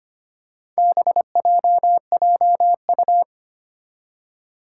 Morse code, heard as a keyed tone, about 25 wpm, 700 Hz. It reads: BJJU
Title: Four Challenging Letter-Number Combinations 25wpm